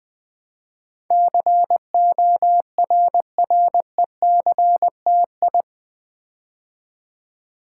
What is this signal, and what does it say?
Morse code, heard as a keyed tone, about 20 wpm, 700 Hz. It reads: CORRECTI
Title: Build A Word - 4 Letters At A Time - 10-Letter Words - 20wpm